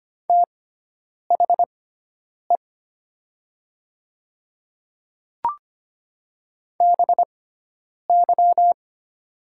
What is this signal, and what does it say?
Morse code, heard as a keyed tone, about 25 wpm, spaced as for 9 wpm, 700 Hz. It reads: THE BY
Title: 6x Character Spacing - Top 100 Words - 25wpm